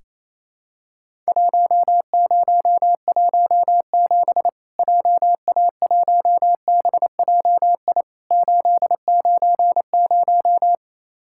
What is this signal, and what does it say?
Morse code, heard as a keyed tone, about 28 wpm, 700 Hz. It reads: 1017 JA1BJI 890
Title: Worked All Europe DX Contest - QTC Exchange - 28wpm